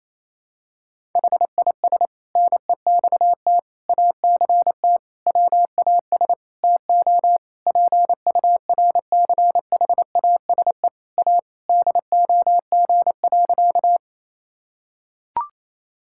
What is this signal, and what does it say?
Morse code, heard as a keyed tone, about 28 wpm, 700 Hz. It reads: HIS NEXT ACT WAS TO PURCHASE A DOG.
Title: Sentences from Top 1000 Words - Review Part 5 - 28wpm